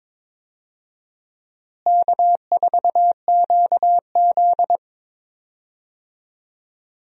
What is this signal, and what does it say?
Morse code, heard as a keyed tone, about 22 wpm, 700 Hz. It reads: K4QZ
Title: USA Call Signs - 4 Characters - Repeated 2x - 22wpm